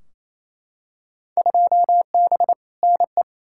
Morse code, heard as a keyed tone, about 28 wpm, 700 Hz. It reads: 2B NE
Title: Field Day Exchanges Encore - 28wpm